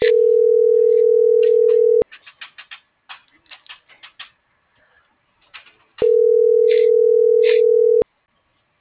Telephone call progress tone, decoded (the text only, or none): ringback tone